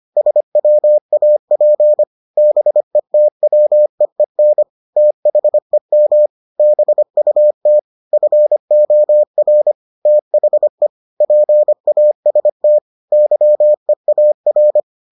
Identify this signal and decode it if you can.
Morse code, heard as a keyed tone, about 25 wpm, 600 Hz. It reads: SWAP BETWEEN THEM BUT FOR THE PAST YEAR